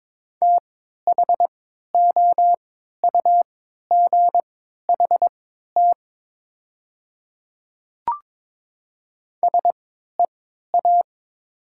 Morse code, heard as a keyed tone, about 22 wpm, spaced as for 13 wpm, 700 Hz. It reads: THOUGHT SEA
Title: 3x Character Spacing - Top 500 Words - 22wpm